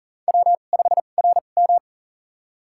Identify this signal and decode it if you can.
Morse code, heard as a keyed tone, about 40 wpm, spaced as for 29 wpm, 700 Hz. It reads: W5RM
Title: USA Call Signs - 4 Characters - 2x Character Spacing - 40wpm